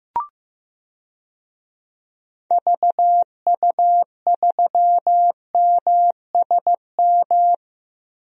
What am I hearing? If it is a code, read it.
Morse code, heard as a keyed tone, about 15 wpm, 700 Hz. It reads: VU3MSM
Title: DXCC-100 Call Signs - 6 Characters - Repeated 3x - 15wpm